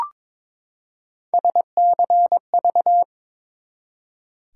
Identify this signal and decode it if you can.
Morse code, heard as a keyed tone, about 22 wpm, 700 Hz. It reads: SCV
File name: ARRL Sections v2 - 22wpm